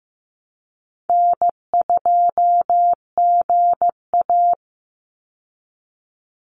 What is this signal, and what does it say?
Morse code, heard as a keyed tone, about 15 wpm, 700 Hz. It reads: N2GA
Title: Easy Call Signs Repeated 2x - 15wpm